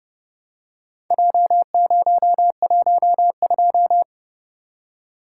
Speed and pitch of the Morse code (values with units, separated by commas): 30 wpm, 700 Hz